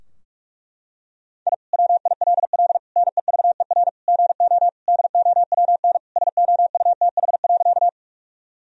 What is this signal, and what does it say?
Morse code, heard as a keyed tone, about 41 wpm, 700 Hz. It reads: I WILL NEVER GO DOWN SOUTH.
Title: Sentences from Top 1000 Words - Review Part 2 - 45wpm